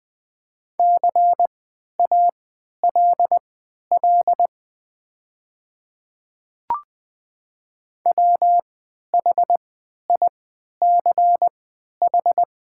Morse code, heard as a keyed tone, about 20 wpm, spaced as for 11 wpm, 700 Hz. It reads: CALL WHICH
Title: 3x Character Spacing - Top 300 Words - 20wpm